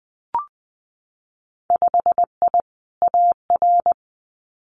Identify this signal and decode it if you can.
Morse code, heard as a keyed tone, about 19 wpm, 700 Hz. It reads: 5I AR
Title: Winter Field Day Exchanges Encore 20wpm